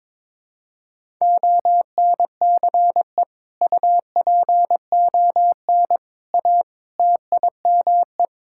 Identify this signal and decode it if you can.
Morse code, heard as a keyed tone, about 22 wpm, 700 Hz. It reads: ONCE UPON A TIME
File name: Sentences with 7-Word Phrases - 22wpm